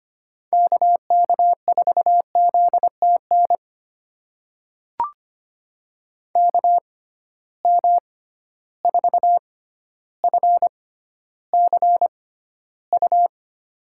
Morse code, heard as a keyed tone, about 25 wpm, spaced as for 13 wpm, 700 Hz. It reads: KK4ZTN KM4FCU
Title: USA Call Signs - 6 Characters - 6x Character Spacing - 25wpm